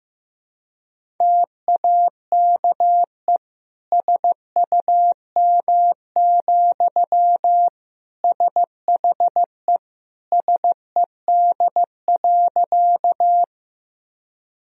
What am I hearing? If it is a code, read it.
Morse code, heard as a keyed tone, about 15 wpm, 700 Hz. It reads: TAKE SUM, SHE SED.
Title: Sentences from Top 200 Words and Abbreviations 15wpm